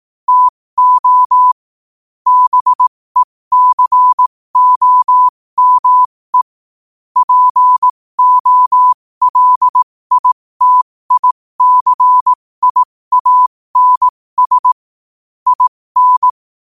Morse code, heard as a keyed tone, about 18 wpm, spaced as for 15 wpm, 1000 Hz. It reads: TO BECOME POLITICIANS IN
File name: qotdfiles / 15wpm